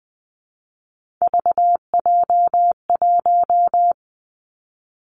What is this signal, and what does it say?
Morse code, heard as a keyed tone, about 20 wpm, 700 Hz. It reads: VJ1
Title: Three Challenging Letter-Number Combinations 20wpm